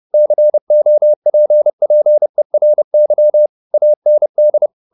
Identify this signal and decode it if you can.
Morse code, heard as a keyed tone, about 30 wpm, 600 Hz. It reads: COPPERY AND